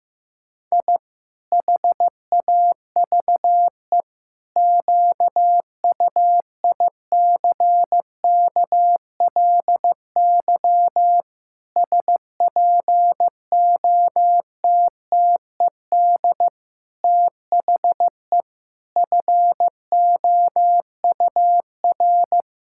Morse code, heard as a keyed tone, about 16 wpm, 700 Hz. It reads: I HAVE QUICKLY SPOTTED THE FOUR
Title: Pangram Challenge v2 - 15wpm